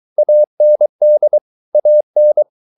Morse code, heard as a keyed tone, about 23 wpm, 600 Hz. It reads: AND AN